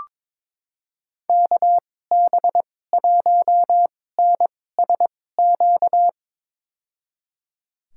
Morse code, heard as a keyed tone, about 22 wpm, spaced as for 16 wpm, 700 Hz. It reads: KB1NSQ
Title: USA Call Signs - 2x Character Spacing - 22wpm